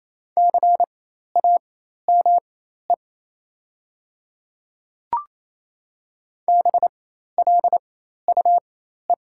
Morse code, heard as a keyed tone, about 28 wpm, spaced as for 13 wpm, 700 Hz. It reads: CAME BLUE
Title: 4x Character Spacing - Top 500 Words - 28wpm